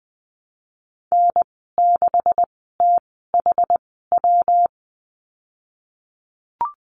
Morse code, heard as a keyed tone, about 20 wpm, spaced as for 15 wpm, 700 Hz. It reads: N6THW
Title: 140 Lesson 23g - Letter f - Call Signs 20wpm